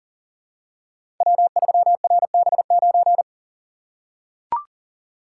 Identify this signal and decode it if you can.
Morse code, heard as a keyed tone, about 39 wpm, 700 Hz. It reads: W3RB9
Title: Code Groups - 5 Letter:Numbers - 40wpm